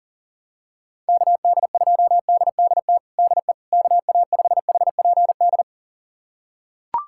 Morse code, heard as a keyed tone, about 40 wpm, 700 Hz. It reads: KD2DDT DE KA5HPD
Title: USA Callsign DE Callsign - 6 Characters - 40wpm